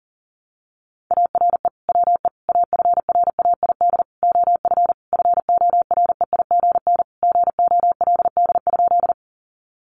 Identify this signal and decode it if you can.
Morse code, heard as a keyed tone, about 40 wpm, 700 Hz. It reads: ARE WE AFRAID OF FOREIGN GOLD?
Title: Sentences from Top 1500 Words 40wpm